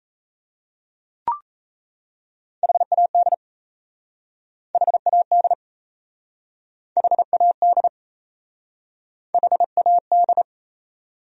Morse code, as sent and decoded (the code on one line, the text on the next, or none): .... .- -.. / .... .- -.. / .... .- -.. / .... .- -..
HAD HAD HAD HAD